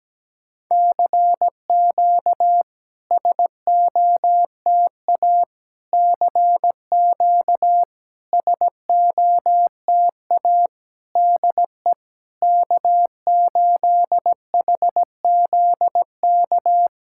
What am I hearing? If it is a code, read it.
Morse code, heard as a keyed tone, about 17 wpm, 700 Hz. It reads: CQ SOTA CQ SOTA DE K8HZK
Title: SOTA QSO Practice 17wpm